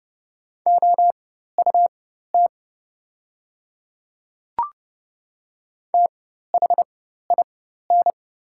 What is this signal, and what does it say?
Morse code, heard as a keyed tone, about 30 wpm, spaced as for 14 wpm, 700 Hz. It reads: OUT THIN
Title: 4x Character Spacing - Top 200 Words - 30wpm